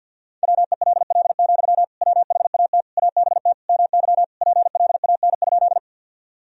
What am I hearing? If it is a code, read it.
Morse code, heard as a keyed tone, about 48 wpm, 700 Hz. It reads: WELL, WHAT ABT MY PLAN?
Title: Sentences from Top 500 Words and Abbreviations 50wpm